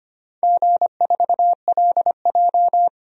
Morse code, heard as a keyed tone, about 25 wpm, 700 Hz. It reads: G4LJ
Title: DXCC-100 Call Signs - 4 Characters - 25wpm